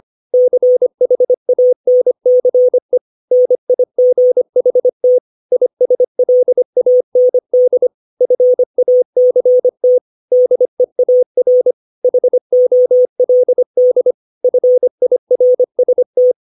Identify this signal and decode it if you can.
Morse code, heard as a keyed tone, about 25 wpm, 500 Hz. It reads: CHANCE NIGHT ISLAND FACT DEAR HOLD FIRST
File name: 632 words / 632 Words 25WPM